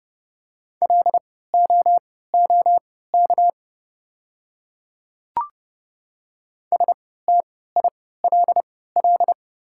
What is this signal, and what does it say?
Morse code, heard as a keyed tone, about 30 wpm, spaced as for 17 wpm, 690 Hz. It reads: LOOK STILL
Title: 3x Character Spacing - Top 300 Words - 30wpm